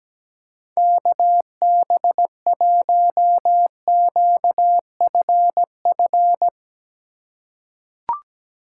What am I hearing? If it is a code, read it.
Morse code, heard as a keyed tone, about 17 wpm, 700 Hz. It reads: KB1QFF
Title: USA Call Signs - 6 Characters - 4x Character Spacing - 17wpm